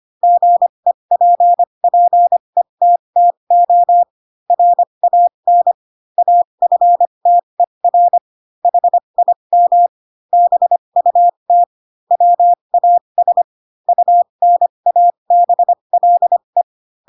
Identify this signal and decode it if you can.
Morse code, heard as a keyed tone, about 25 wpm, spaced as for 22 wpm, 700 Hz. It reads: GEPPETTO RAN AFTER HIM BUT WAS UNABLE
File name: Pinocchio / Pinocchio ch3 18 25 ik1hge